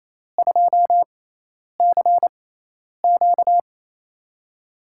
Morse code, heard as a keyed tone, about 28 wpm, spaced as for 10 wpm, 700 Hz. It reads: 2CQ
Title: USA Call Signs - 5 Characters - 6x Character Spacing - 28wpm